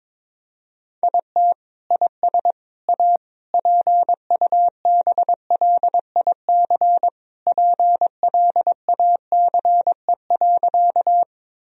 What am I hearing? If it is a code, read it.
Morse code, heard as a keyed tone, about 21 wpm, 700 Hz. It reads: IT IS A PUBLIC PLACE.
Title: Sentences from Top 1000 Words - Review Part 3 - 22wpm